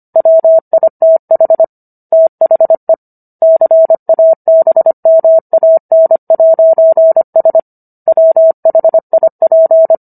Morse code, heard as a keyed tone, about 25 wpm, 650 Hz. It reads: WITH THE CABMAN'S WHIP